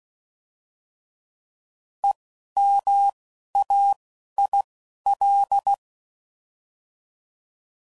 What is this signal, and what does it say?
Morse code, heard as a keyed tone, about 16 wpm, spaced as for 11 wpm, 800 Hz. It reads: EMAIL